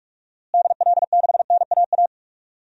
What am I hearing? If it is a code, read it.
Morse code, heard as a keyed tone, about 46 wpm, 700 Hz. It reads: DL6NAA
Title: DXCC-100 Call Signs - 6 Characters - Repeated 3x - 45wpm